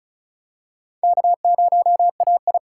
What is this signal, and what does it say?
Morse code, heard as a keyed tone, about 36 wpm, 700 Hz. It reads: K0AI
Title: USA Call Signs - 4 Characters - 3x Character Spacing - 35wpm